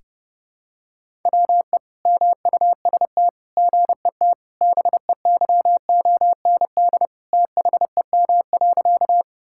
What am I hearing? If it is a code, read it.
Morse code, heard as a keyed tone, about 28 wpm, 700 Hz. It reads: WE MUST GET BEYOND THEM.
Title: Sentences from Top 1000 Words - Review Part 6 - 30wpm